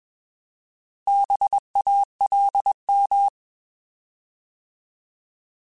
Morse code, heard as a keyed tone, about 21 wpm, 800 Hz. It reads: BALM
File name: Bs @20